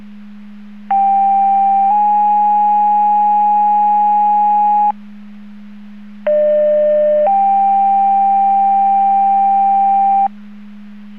Two-tone page, heard as a one-second tone, then a three-second tone, about 790 then 830 Hz, about 600 then 790 Hz. It is heard from 0.9 to 10.3 s.